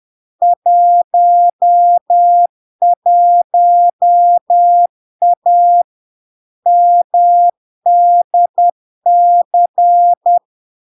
Morse code, heard as a keyed tone, about 10 wpm, 700 Hz. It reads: 11A MDC
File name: LARK-FD-2022-Log-in-CW-10wpm-700-Hz